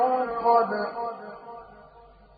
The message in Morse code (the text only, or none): none